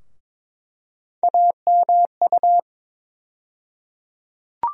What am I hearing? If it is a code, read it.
Morse code, heard as a keyed tone, about 22 wpm, 700 Hz. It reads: AMU
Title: USA Call Sign Postfixes - 3 Letters 22wpm